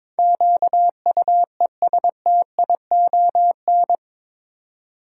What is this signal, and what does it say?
Morse code, heard as a keyed tone, about 22 wpm, 700 Hz. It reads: QUESTION
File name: practice / 22wpm / Eight Letter Words - 22wpm